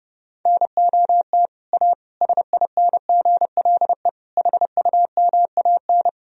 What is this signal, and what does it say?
Morse code, heard as a keyed tone, about 30 wpm, 700 Hz. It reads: NOT A SINGLE HUMAN
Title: Sentences from Top 1000 Words - Review Part 4 - 30wpm